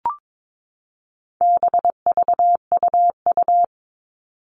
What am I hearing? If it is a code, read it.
Morse code, heard as a keyed tone, about 22 wpm, 700 Hz. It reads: BVUU
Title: All Cut Numbers - 4-Digit 22wpm